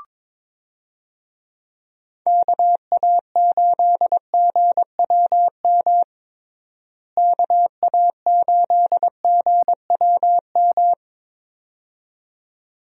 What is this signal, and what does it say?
Morse code, heard as a keyed tone, about 22 wpm, 700 Hz. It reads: KA8GWM KA8GWM
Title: USA Call Signs - 6 Characters - Repeated 2x - 22wpm